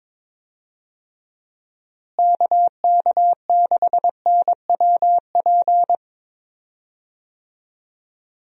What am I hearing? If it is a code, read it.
Morse code, heard as a keyed tone, about 22 wpm, 700 Hz. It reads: KK6NWP
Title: USA Call Signs - 6 Characters - Repeated 2x - 22wpm